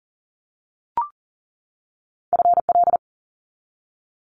Morse code, heard as a keyed tone, about 40 wpm, 700 Hz. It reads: FL
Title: Multi-letter Phonemes 40wpm